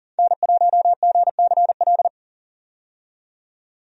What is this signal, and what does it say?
Morse code, heard as a keyed tone, about 40 wpm, 700 Hz. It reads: N1GCL
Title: USA Call Signs - 5 Characters - 40wpm